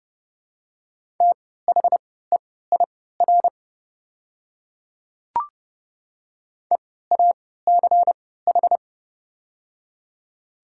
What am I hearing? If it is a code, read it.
Morse code, heard as a keyed tone, about 30 wpm, spaced as for 17 wpm, 700 Hz. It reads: THEIR EACH
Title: 3x Character Spacing - Top 100 Words - 30wpm